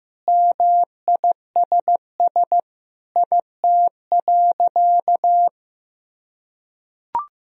Morse code, heard as a keyed tone, about 15 wpm, 700 Hz. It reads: MISS IT.
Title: Sentences from Top 1000 Words - Review Part 3 - 15wpm